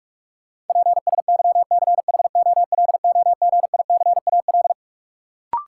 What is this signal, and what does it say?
Morse code, heard as a keyed tone, about 41 wpm, 700 Hz. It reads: PSYCHOLOGICAL
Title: Sentences from Top 1400 Words - New Word Drill Alt - 45wpm